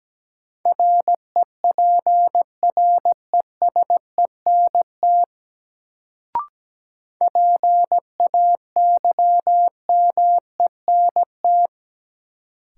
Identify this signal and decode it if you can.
Morse code, heard as a keyed tone, about 17 wpm, 700 Hz. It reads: REPRESENT PAYMENT
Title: Sentences from Top 1200 Words - New Word Drill - 17wpm